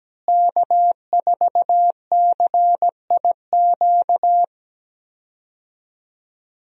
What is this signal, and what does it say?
Morse code, heard as a keyed tone, about 17 wpm, 700 Hz. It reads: K4CIQ
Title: USA Call Signs - 5 Characters - Repeated 3x - 17wpm